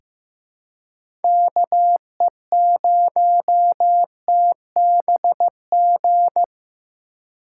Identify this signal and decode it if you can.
Morse code, heard as a keyed tone, about 15 wpm, 700 Hz. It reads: KE0TBG
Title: USA Call Signs - 6 Characters - 6x Character Spacing - 15wpm